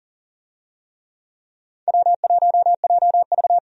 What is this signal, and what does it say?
Morse code, heard as a keyed tone, about 39 wpm, 700 Hz. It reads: W1JV
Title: USA Call Signs - 4 Characters - 40wpm